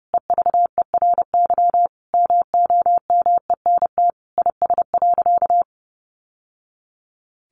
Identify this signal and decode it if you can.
Morse code, heard as a keyed tone, about 30 wpm, 700 Hz. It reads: EVERY MOMENT IS.
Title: Sentences from Top 1000 Words 30wpm